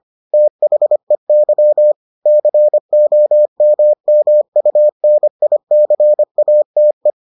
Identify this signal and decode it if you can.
Morse code, heard as a keyed tone, about 25 wpm, 600 Hz. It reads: THEY COMMUNICATE